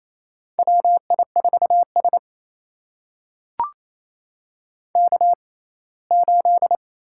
Morse code, heard as a keyed tone, about 28 wpm, spaced as for 19 wpm, 700 Hz. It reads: WI4S K8
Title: USA Call Signs - 4 Characters - 6x Character Spacing - 28wpm